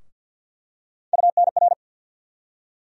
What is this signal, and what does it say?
Morse code, heard as a keyed tone, about 47 wpm, 700 Hz. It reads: UNR
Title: USA Call Sign Postfixes - 3 Letters 50wpm